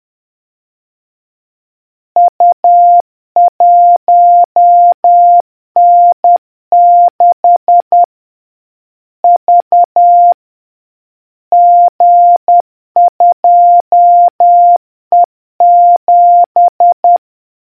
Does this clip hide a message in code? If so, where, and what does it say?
Yes: Morse code from 2.2 to 17.2 s, heard as a keyed tone, about 10 wpm, 700 Hz. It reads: U1N6 V G2E7